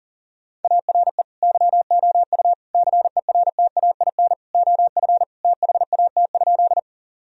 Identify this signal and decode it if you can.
Morse code, heard as a keyed tone, about 40 wpm, 700 Hz. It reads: ARE YOU CERTAIN OF THAT?
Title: Sentences from Top 1000 Words - Review Part 2 - 40wpm